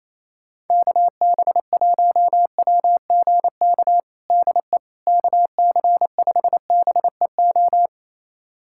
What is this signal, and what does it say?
Morse code, heard as a keyed tone, about 27 wpm, 700 Hz. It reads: KB1WGK DE KC5BEO
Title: USA Callsign DE Callsign - 6 Characters - 28wpm